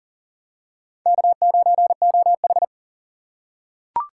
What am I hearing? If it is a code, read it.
Morse code, heard as a keyed tone, about 40 wpm, 700 Hz. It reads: K9OH
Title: USA Call Signs - 6x Character Spacing - 40wpm